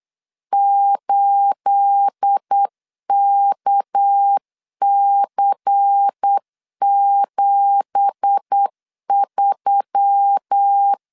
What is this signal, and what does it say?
Morse code, heard as a keyed tone, about 8 wpm, 800 Hz. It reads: 8KC73
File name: LEZIONI / 10 SIGNS / 10A SLASH 8wpm 3452